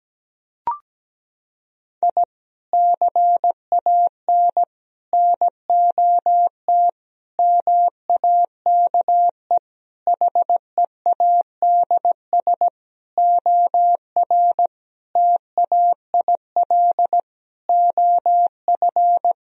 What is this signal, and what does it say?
Morse code, heard as a keyed tone, about 17 wpm, 700 Hz. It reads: I CAN NOT MAKE HEADS OR TAIL OF
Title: Sentences from Top 1000 Words - Review Part 3 - 17wpm